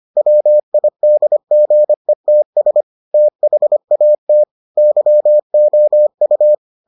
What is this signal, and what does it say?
Morse code, heard as a keyed tone, about 25 wpm, 600 Hz. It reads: WIDGETS THAT YOU